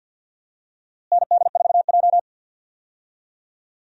Morse code, heard as a keyed tone, about 47 wpm, 700 Hz. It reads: ND4J
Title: USA Call Signs - 4 Characters - Repeated 3x - 50wpm